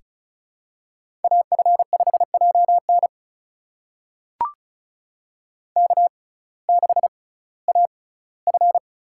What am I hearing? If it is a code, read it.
Morse code, heard as a keyed tone, about 36 wpm, spaced as for 19 wpm, 700 Hz. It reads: AF5JN K6AF
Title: USA Call Signs - 5 Characters - 6x Character Spacing - 35wpm